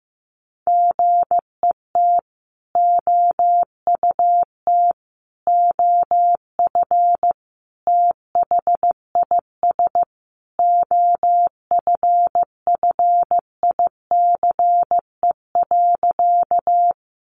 Morse code, heard as a keyed tone, about 15 wpm, 700 Hz. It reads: GET OUT OF THIS OFFICE.
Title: Sentences from Top 1000 Words 15wpm